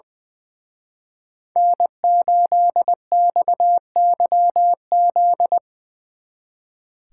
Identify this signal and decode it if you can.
Morse code, heard as a keyed tone, about 22 wpm, 700 Hz. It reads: N8XYZ
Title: Speed Racing - USA 1x3 Call Signs 20wpm